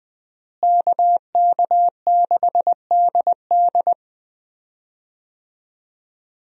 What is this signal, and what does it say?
Morse code, heard as a keyed tone, about 20 wpm, 700 Hz. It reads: KK6DD
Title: USA Call Signs - 5 Characters - Repeated 2x - 20wpm